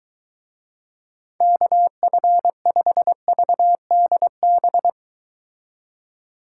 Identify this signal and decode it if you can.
Morse code, heard as a keyed tone, about 23 wpm, 700 Hz. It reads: KF5VDB